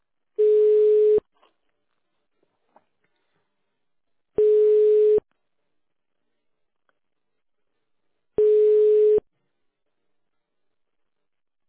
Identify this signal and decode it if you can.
Ringback tone (425 Hz, 1 s on, 3 s off).